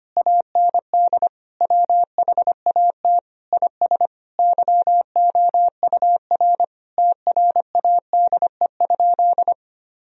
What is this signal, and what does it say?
Morse code, heard as a keyed tone, about 25 wpm, 700 Hz. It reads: AND WHAT IS YOUR TRADE?
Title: Sentences from Top 1000 Words - Review Part 4 - 25wpm